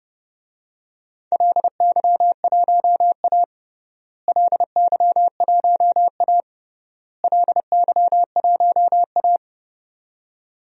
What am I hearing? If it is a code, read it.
Morse code, heard as a keyed tone, about 30 wpm, 700 Hz. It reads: LY1A LY1A LY1A
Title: DXCC-100 Call Signs - 4 Characters - Repeated 3x - 30wpm